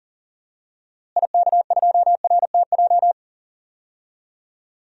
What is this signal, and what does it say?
Morse code, heard as a keyed tone, about 40 wpm, 700 Hz. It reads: IK2RTJ
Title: DXCC-100 Call Signs - 6 Characters - Repeated 2x - 40wpm